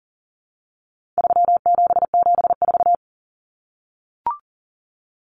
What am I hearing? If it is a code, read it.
Morse code, heard as a keyed tone, about 40 wpm, 700 Hz. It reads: 3774
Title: Common Cut Numbers - 4-Digit 40wpm